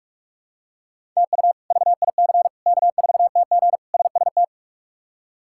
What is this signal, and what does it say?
Morse code, heard as a keyed tone, about 44 wpm, 700 Hz. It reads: TU VIC K4TG SST
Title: K1USN SST QSO Practice - 45wpm